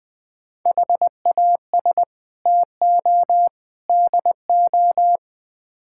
Morse code, heard as a keyed tone, about 20 wpm, 690 Hz. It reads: HAS TO DO
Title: Build A Phrase - 4-Word Phrases - 20wpm